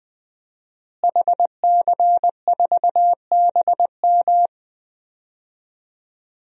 Morse code, heard as a keyed tone, about 21 wpm, 700 Hz. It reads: HC4BM
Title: DXCC-100 Call Signs - 5 Characters - Repeated 3x - 20wpm